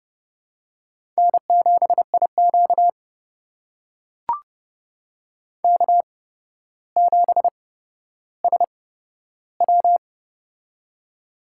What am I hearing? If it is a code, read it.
Morse code, heard as a keyed tone, about 30 wpm, spaced as for 13 wpm, 700 Hz. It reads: N7IQ K7SW
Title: USA Call Signs - 4 Characters - 8x Character Spacing - 30wpm